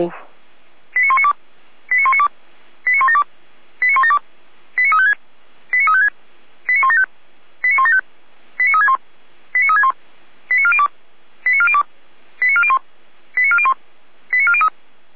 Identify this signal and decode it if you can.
Five-tone selective call (ZVEI), heard as quick runs of five tones, tones of about 70 ms, with speech. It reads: 89192 89192 89172 89172 89356 89356 89175 89175 89361 89361 89402 89402 89401 89401 89403